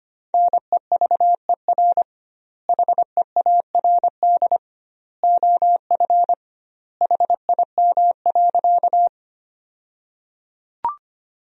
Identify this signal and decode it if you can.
Morse code, heard as a keyed tone, about 25 wpm, 700 Hz. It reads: NEVER HEARD OF HIM.
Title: 2x Word-Spacing - Sentences from Top 400 Words - 25wpm